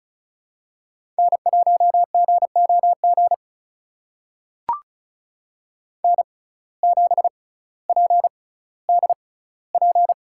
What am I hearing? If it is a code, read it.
Morse code, heard as a keyed tone, about 34 wpm, spaced as for 18 wpm, 700 Hz. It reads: N1GOG N7PDP
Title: USA Call Signs - 5 Characters - 6x Character Spacing - 35wpm